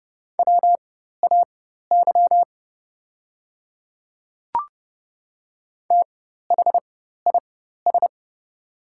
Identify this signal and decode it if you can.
Morse code, heard as a keyed tone, about 30 wpm, spaced as for 14 wpm, 700 Hz. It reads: WAY THIS
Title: 4x Character Spacing - Top 100 Words - 30wpm